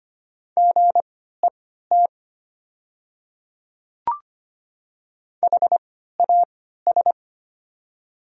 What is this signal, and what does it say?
Morse code, heard as a keyed tone, about 25 wpm, spaced as for 14 wpm, 700 Hz. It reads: GET HAS
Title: 3x Character Spacing - Top 200 Words - 25wpm